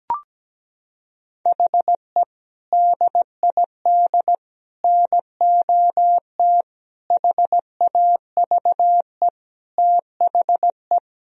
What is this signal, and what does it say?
Morse code, heard as a keyed tone, about 17 wpm, 700 Hz. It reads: HE DID NOT HAVE THE
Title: Sentences from Top 1000 Words - Review Part 1 - 17wpm